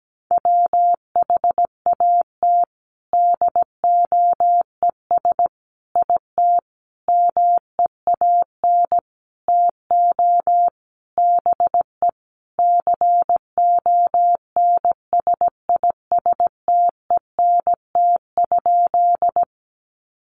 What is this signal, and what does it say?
Morse code, heard as a keyed tone, about 17 wpm, 700 Hz. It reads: WHAT DOES IT MEAN TO BE CONSISTENT?
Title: Sentences from Top 1500 Words 17wpm